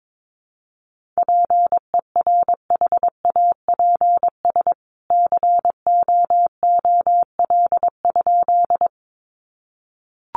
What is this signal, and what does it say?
Morse code, heard as a keyed tone, about 22 wpm, 700 Hz. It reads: PERHAPS COOL?
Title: Sentences from Top 1200 Words 22wpm